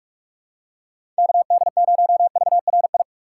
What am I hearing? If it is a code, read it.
Morse code, heard as a keyed tone, about 47 wpm, 700 Hz. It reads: KD0VRI
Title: USA Call Signs - 6 Characters - 4x Character Spacing - 45wpm